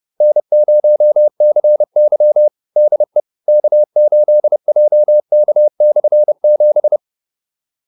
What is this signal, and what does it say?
Morse code, heard as a keyed tone, about 30 wpm, 600 Hz. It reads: N0CY DE K8JK/7